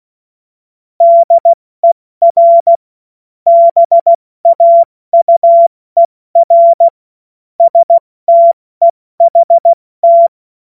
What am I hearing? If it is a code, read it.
Morse code, heard as a keyed tone, about 16 wpm, 680 Hz. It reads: DER BAUER STEHT
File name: Morsekurs / Witze / Witz 17 Juni Mai 2025 70 F80